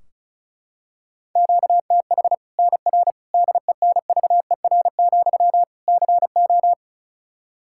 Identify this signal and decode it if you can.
Morse code, heard as a keyed tone, about 36 wpm, 700 Hz. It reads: QTH NR DENVER, CO
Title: Practice QSO - Exchange 1 - Beginning - 35wpm